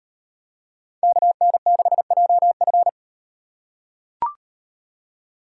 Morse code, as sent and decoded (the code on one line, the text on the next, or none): -.- -. -.... .--- ..-.
KN6JF